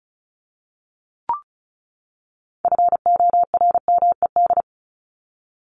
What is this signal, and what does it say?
Morse code, heard as a keyed tone, about 34 wpm, 700 Hz. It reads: FORMED
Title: ED Suffix - Top 100 Words with Base Words 35wpm